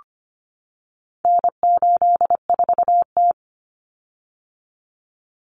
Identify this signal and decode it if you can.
Morse code, heard as a keyed tone, about 25 wpm, 700 Hz. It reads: N84T
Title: Common Cut Numbers - 4-Digit 25wpm